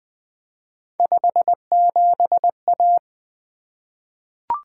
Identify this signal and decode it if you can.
Morse code, heard as a keyed tone, about 20 wpm, 700 Hz. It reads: 57A
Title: Three Letter-Number Combinations - 20wpm